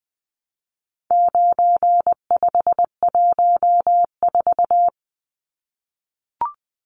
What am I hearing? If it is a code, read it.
Morse code, heard as a keyed tone, about 20 wpm, 700 Hz. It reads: 9514
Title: Numbers - 4-Digit 20wpm